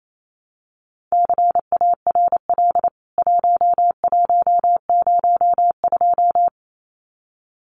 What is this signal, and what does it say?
Morse code, heard as a keyed tone, about 28 wpm, 700 Hz. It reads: CARL 1102
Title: CWT Exchange 28wpm